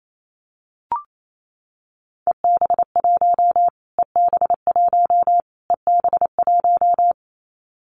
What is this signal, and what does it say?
Morse code, heard as a keyed tone, about 28 wpm, 700 Hz. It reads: EB1 EB1 EB1
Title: Callsign Prefixes for Top 25 DXCC - Repeated 3x - 28wpm